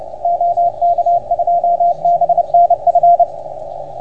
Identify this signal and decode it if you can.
Morse code, heard as a keyed tone, about 30 wpm, 690 Hz, with radio noise. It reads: OK2BNF
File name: Archiv / CUC381 / OK1FGD CUC 381